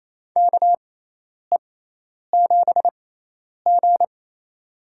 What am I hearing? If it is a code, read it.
Morse code, heard as a keyed tone, about 28 wpm, spaced as for 10 wpm, 700 Hz. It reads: KE7G
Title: USA Call Signs - 4 Characters - 6x Character Spacing - 28wpm